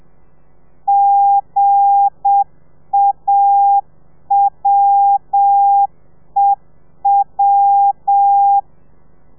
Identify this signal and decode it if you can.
Morse code, heard as a keyed tone, about 7 wpm, 800 Hz. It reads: GAWEW
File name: telegrafia / i2tfj / Lez8 35